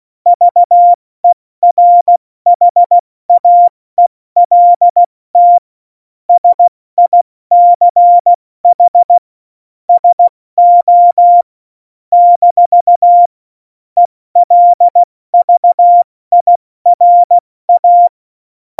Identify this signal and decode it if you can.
Morse code, heard as a keyed tone, about 16 wpm, spaced as for 14 wpm, 700 Hz. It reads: VERHAELT SICH SO - ELVIRA
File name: Morsekurs / Witze / Witz 25. Juni 2019, 70 F80